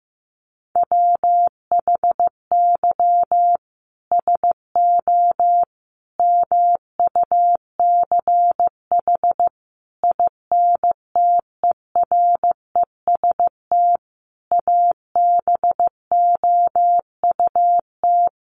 Morse code, heard as a keyed tone, about 15 wpm, 700 Hz. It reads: WHY SO MUCH INTEREST ABOUT
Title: Sentences from Top 1200 Words 15wpm